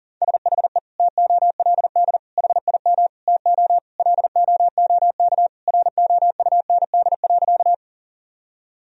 Morse code, heard as a keyed tone, about 40 wpm, 700 Hz. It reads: SHE TOLD HIM TO LOOK ROUND.
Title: Sentences from Top 1000 Words - Review Part 1 - 40wpm